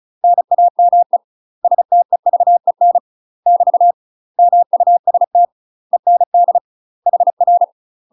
Morse code, heard as a keyed tone, about 36 wpm, 700 Hz. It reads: NAME STEVEN <BT> MUST END HR
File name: UK MASTER QSOs TRAINING / 35WPM / UK MASTER QSOs 0009